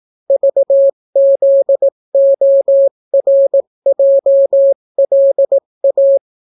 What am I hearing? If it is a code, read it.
Morse code, heard as a keyed tone, about 18 wpm, 550 Hz. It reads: VZORJLA